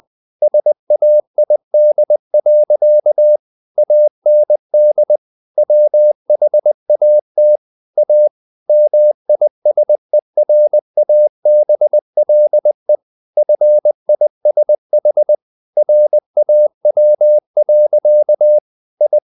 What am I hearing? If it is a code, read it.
Morse code, heard as a keyed tone, about 20 wpm, 600 Hz. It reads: SAID. AND WHAT A MISERABLE FISH RAW. I
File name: the old man and the sea part 2 20wpm